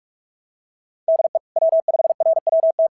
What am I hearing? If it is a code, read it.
Morse code, heard as a keyed tone, about 45 wpm, 650 Hz. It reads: DE W5RWT
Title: SOTA QSO Practice 45wpm